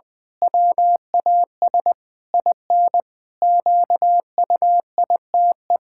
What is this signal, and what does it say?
Morse code, heard as a keyed tone, about 21 wpm, 700 Hz. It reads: WAS IN QUITE
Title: Pangram Challenge v2 - 20wpm